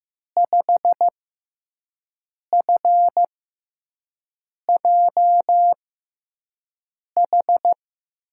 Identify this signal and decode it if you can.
Morse code, heard as a keyed tone, about 15 wpm, spaced as for 5 wpm, 700 Hz. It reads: 5FJH
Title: USA Call Signs - 6 Characters - 6x Character Spacing - 15wpm